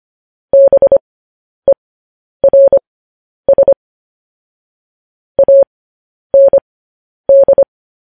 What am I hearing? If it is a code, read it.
Morse code, heard as a keyed tone, about 25 wpm, spaced as for 10 wpm, 550 Hz. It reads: BERS AND
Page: news headlines in morse code at 10 wpm